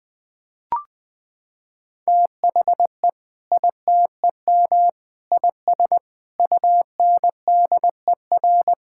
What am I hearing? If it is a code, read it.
Morse code, heard as a keyed tone, about 20 wpm, 700 Hz. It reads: THE ITEM IS UNDER
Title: Sentences from Top 1000 Words - Review Part 6 - 20wpm